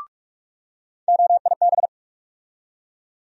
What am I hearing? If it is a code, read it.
Morse code, heard as a keyed tone, about 45 wpm, 700 Hz. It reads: OIB
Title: USA Call Sign Postfixes - 3 Letters 45wpm